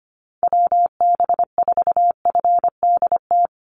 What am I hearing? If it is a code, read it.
Morse code, heard as a keyed tone, about 25 wpm, 700 Hz. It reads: WB4FDT
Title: Call Signs Repeated 3x - 25wpm